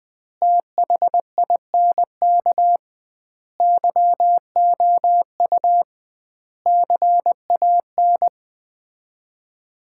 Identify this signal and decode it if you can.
Morse code, heard as a keyed tone, about 21 wpm, 700 Hz. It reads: THINK YOU CAN
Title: 2x Word-Spacing - Sets of 5 Words - 20wpm